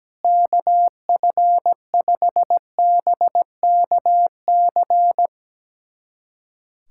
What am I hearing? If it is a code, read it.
Morse code, heard as a keyed tone, about 17 wpm, 700 Hz. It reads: KF5BKC
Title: USA Callsign DE Callsign - 6 Characters - 17wpm